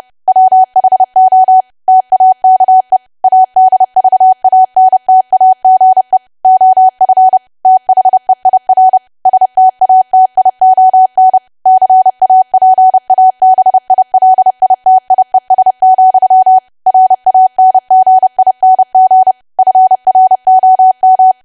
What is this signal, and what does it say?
Morse code, heard as a keyed tone, about 30 wpm, 750 Hz. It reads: WHO TAKE ADVANTAGE OF THEIR STATION CAPABILITIES, RANGING FROM
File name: cwa / W1AW-Level3 / 30WPM